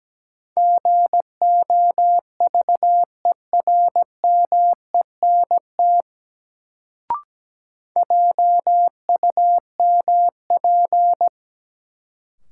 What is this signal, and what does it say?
Morse code, heard as a keyed tone, about 17 wpm, 700 Hz. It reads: GOVERMENT JUMP
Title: Sentences from Top 1300 Words - New Word Drill - 17wpm